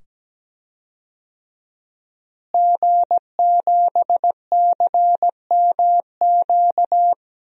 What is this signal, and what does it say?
Morse code, heard as a keyed tone, about 17 wpm, 700 Hz. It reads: G7CMQ
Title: DXCC-100 Call Signs - 5 Characters - Repeated 2x - 17wpm